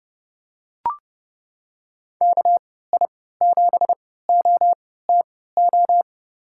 Morse code, heard as a keyed tone, about 30 wpm, spaced as for 17 wpm, 700 Hz. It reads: KI7OTO
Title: USA Call Signs - 6 Characters - 3x Character Spacing - 30wpm